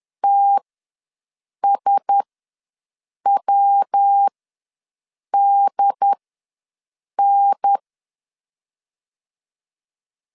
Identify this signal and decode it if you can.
Morse code, heard as a keyed tone, about 10 wpm, spaced as for 6 wpm, 800 Hz. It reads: TSWDN
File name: LEZIONI / 5 LBVC / 5 L ws 10wpm 3636